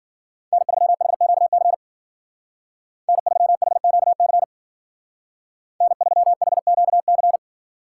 Morse code, heard as a keyed tone, about 49 wpm, 700 Hz. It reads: N3HQC N3HQC N3HQC
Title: Speed Racing - USA 1x3 Call Signs 40wpm